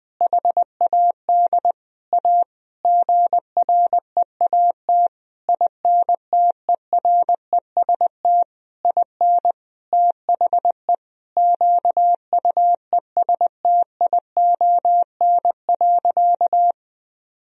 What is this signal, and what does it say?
Morse code, heard as a keyed tone, about 20 wpm, 700 Hz. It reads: HAD A GREAT INTEREST IN THE QUESTION.
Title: Sentences from Top 1000 Words - Review Part 2 - 20wpm